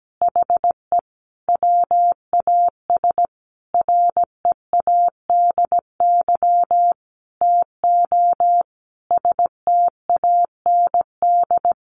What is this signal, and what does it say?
Morse code, heard as a keyed tone, about 17 wpm, 700 Hz. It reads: HE WAS READY TO STAND
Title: Sentences from Top 1400 Words 17wpm